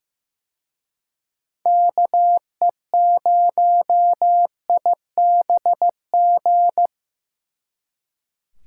Morse code, heard as a keyed tone, about 18 wpm, 700 Hz. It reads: KE0IBG
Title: Speed Racing - USA 2x3 Call Signs 15wpm